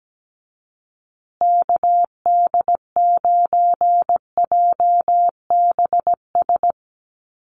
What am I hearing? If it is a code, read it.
Morse code, heard as a keyed tone, about 17 wpm, 700 Hz. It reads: KD9JBS
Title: USA Call Signs Repeated 3x 17wpm